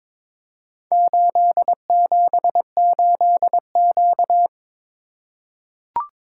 Morse code, heard as a keyed tone, about 22 wpm, 700 Hz. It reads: 878Q
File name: practice / 22wpm / Four Letter-Number Combinations - 22wpm